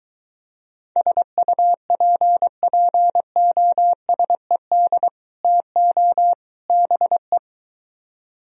Morse code, heard as a keyed tone, about 23 wpm, 700 Hz. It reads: SUPPOSED TO BE